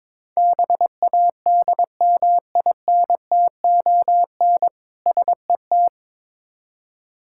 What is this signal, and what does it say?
Morse code, heard as a keyed tone, about 22 wpm, 690 Hz. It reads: BADMINTON SET
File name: Difficult Words with Hints - 22wpm